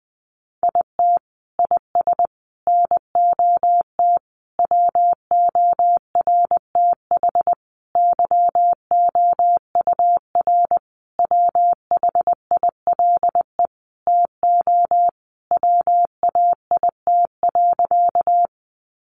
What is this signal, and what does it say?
Morse code, heard as a keyed tone, about 19 wpm, 700 Hz. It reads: IT IS NOT WORTH YOUR WHILE TO WAIT.
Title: Sentences from Top 800 Words 20wpm